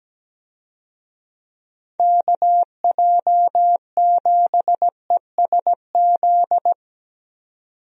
Morse code, heard as a keyed tone, about 17 wpm, 700 Hz. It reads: KJ7ESZ
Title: USA Call Signs - 6 Characters - 17wpm